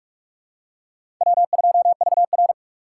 Morse code, heard as a keyed tone, about 45 wpm, 700 Hz. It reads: W2VR
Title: USA Call Signs - 4x Character Spacing - 45wpm